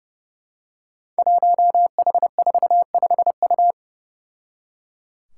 Morse code, heard as a keyed tone, about 30 wpm, 700 Hz. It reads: 1H45U
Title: Five Challenging Letter-Number Combinations 30wpm